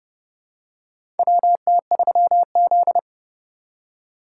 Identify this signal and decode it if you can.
Morse code, heard as a keyed tone, about 30 wpm, 700 Hz. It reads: WT3Z
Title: USA Call Signs - 4 Characters - 4x Character Spacing - 30wpm